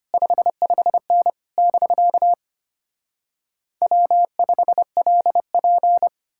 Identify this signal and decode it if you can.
Morse code, heard as a keyed tone, about 33 wpm, 700 Hz. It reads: 55N <BK> W5LP